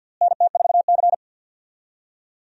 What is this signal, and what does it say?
Morse code, heard as a keyed tone, about 50 wpm, 700 Hz. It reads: NT4C
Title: USA Call Signs - 4 Characters - Repeated 2x - 50wpm